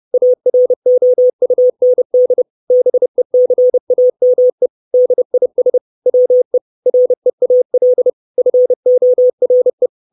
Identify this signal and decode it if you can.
Morse code, heard as a keyed tone, about 30 wpm, 500 Hz. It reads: AROUND BECAME DIS WE REAL FORE
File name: 632 Words 30WPM